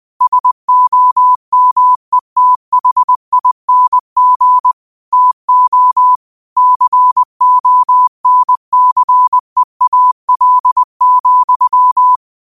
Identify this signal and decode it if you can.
Morse code, heard as a keyed tone, about 20 wpm, 1000 Hz. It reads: SOMETHING TO CONCEAL,